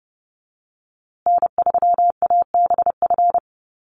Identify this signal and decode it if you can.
Morse code, heard as a keyed tone, about 30 wpm, 700 Hz. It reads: N3ABF
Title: USA Call Signs Repeated 2x 30wpm